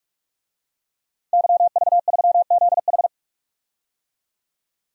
Morse code, heard as a keyed tone, about 46 wpm, 700 Hz. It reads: YV3ZH
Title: DXCC-100 Call Signs - 5 Characters - 45wpm